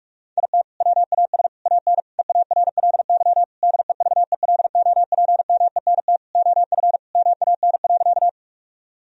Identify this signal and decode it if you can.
Morse code, heard as a keyed tone, about 46 wpm, 700 Hz. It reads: IT WAS AN EARLY DEVELOPMENT OF MAN.
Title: Sentences from Top 1000 Words - Review Part 4 - 45wpm